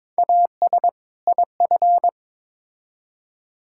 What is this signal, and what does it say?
Morse code, heard as a keyed tone, about 22 wpm, 700 Hz. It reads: AS IF
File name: Build A Phrase - 2-Word Phrases - 22wpm